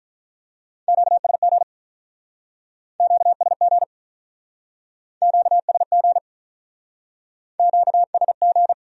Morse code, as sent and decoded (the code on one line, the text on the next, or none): --.- ... --. / --.- ... --. / --.- ... --. / --.- ... --.
QSG QSG QSG QSG